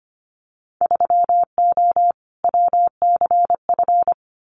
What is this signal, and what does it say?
Morse code, heard as a keyed tone, about 25 wpm, 700 Hz. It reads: 3O WCF
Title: Winter Field Day Exchanges 25wpm